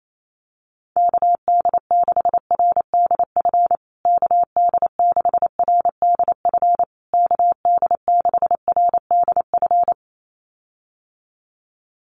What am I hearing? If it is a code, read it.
Morse code, heard as a keyed tone, about 28 wpm, 700 Hz. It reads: KD6RDF KD6RDF KD6RDF
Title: USA Call Signs Repeated 3x 28wpm